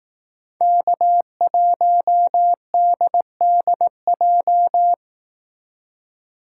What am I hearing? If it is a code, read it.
Morse code, heard as a keyed tone, about 19 wpm, 700 Hz. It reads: K1DDJ
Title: Speed Racing - USA 1x3 Call Signs 15wpm